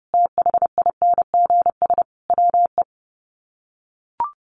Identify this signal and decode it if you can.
Morse code, heard as a keyed tone, about 28 wpm, 700 Hz. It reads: THINGS WE
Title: Sets of 5 Words Encore 30wpm